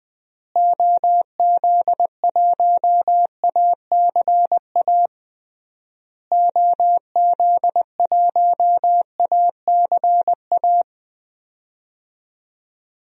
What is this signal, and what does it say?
Morse code, heard as a keyed tone, about 21 wpm, 700 Hz. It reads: OZ1ACA OZ1ACA
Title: DXCC-100 Call Signs - 6 Characters - Repeated 3x - 20wpm